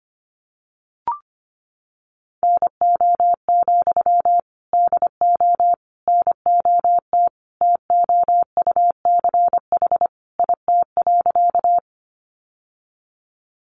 Morse code, heard as a keyed tone, about 25 wpm, 700 Hz. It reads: NO, DO NOT TOUCH IT.
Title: Sentences from Top 1200 Words 25wpm